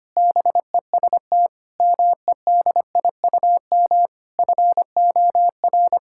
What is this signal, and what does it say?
Morse code, heard as a keyed tone, about 25 wpm, 700 Hz. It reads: BEST MEDIUM FOR
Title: Sentences from Top 1000 Words - Review Part 6 - 25wpm